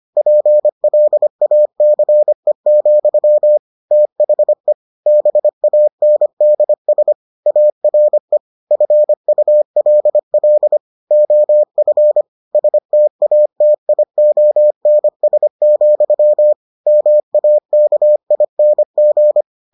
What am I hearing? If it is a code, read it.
Morse code, heard as a keyed tone, about 25 wpm, 600 Hz. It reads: PLACE, THE BANDS ARE FULL OF STATIONS, MAKING